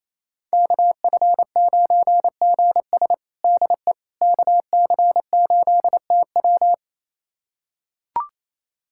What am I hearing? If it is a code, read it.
Morse code, heard as a keyed tone, about 28 wpm, 700 Hz. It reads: KF9GS DE KC8TW
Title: USA Callsign DE Callsign - 5 Characters - 28wpm